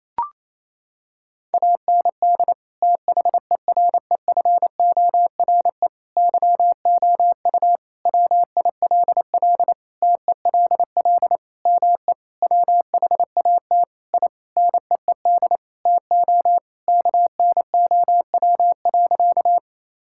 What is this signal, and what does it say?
Morse code, heard as a keyed tone, about 27 wpm, 700 Hz. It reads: AND THEREFORE YOU WILL TELL ME WHAT I NEED TO KNOW.
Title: Sentences from Top 1000 Words - Review Part 5 - 28wpm